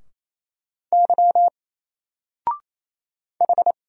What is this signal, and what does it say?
Morse code, heard as a keyed tone, about 27 wpm, 700 Hz. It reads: Y H
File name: Single-Letter-Number-28wpm